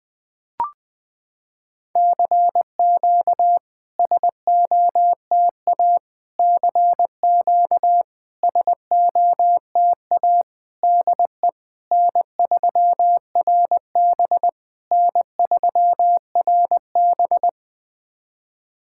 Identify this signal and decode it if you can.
Morse code, heard as a keyed tone, about 20 wpm, 690 Hz. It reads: CQ SOTA CQ SOTA DE N3RB N3RB
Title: SOTA QSO Practice 20wpm